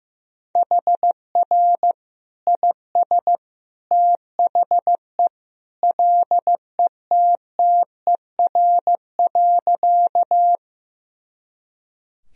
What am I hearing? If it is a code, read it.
Morse code, heard as a keyed tone, about 15 wpm, 700 Hz. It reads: HR IS THE LETTER.
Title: Sentences from Top 300 Words and Abbreviations 15wpm